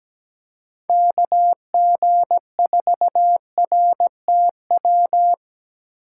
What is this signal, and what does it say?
Morse code, heard as a keyed tone, about 17 wpm, 700 Hz. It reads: KG4RTW
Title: USA Call Signs - 6 Characters - 2x Character Spacing - 17wpm